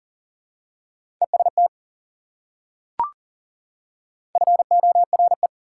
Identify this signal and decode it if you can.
Morse code, heard as a keyed tone, about 40 wpm, 700 Hz. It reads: EST FORE
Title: Prefixes and Suffixes 40wpm